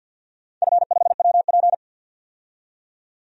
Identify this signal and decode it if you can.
Morse code, heard as a keyed tone, about 53 wpm, 700 Hz. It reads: F5WP
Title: DXCC-100 Call Signs - 4 Characters - 50wpm